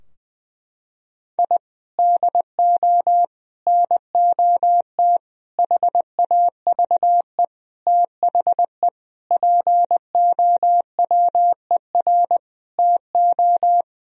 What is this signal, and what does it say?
Morse code, heard as a keyed tone, about 21 wpm, 700 Hz. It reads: I DO NOT HAVE THE POWER TO
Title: Sentences from Top 1000 Words - Review Part 3 - 20wpm